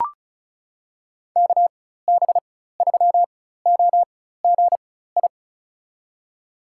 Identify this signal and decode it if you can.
Morse code, heard as a keyed tone, about 34 wpm, spaced as for 16 wpm, 690 Hz. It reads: KB3OGI
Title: USA Call Signs - 4x Character Spacing - 35wpm